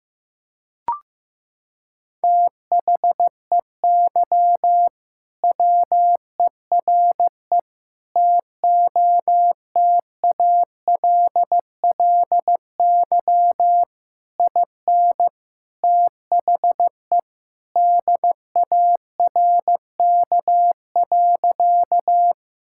Morse code, heard as a keyed tone, about 15 wpm, 700 Hz. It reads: THEY WERE TOTALLY IN THE DARK.
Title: Sentences from Top 1600 Words 15wpm